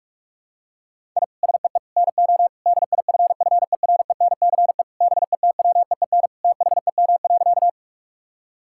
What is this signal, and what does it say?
Morse code, heard as a keyed tone, about 41 wpm, 700 Hz. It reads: I SEE NO DIFFERENCE BETWEEN THEM.
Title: Sentences from Top 1000 Words - Review Part 5 - 45wpm